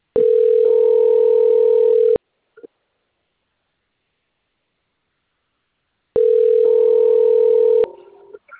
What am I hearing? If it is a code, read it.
Ringback tone (North America), 440 + 480 Hz.